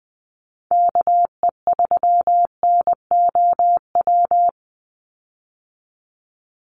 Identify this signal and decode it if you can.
Morse code, heard as a keyed tone, about 20 wpm, 700 Hz. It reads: KE3NOW
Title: USA Call Signs 20wpm